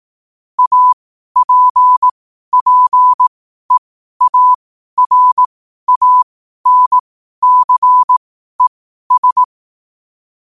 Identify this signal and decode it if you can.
Morse code, heard as a keyed tone, about 18 wpm, spaced as for 12 wpm, 1000 Hz. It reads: APPEARANCES
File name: qotdfiles / 12wpm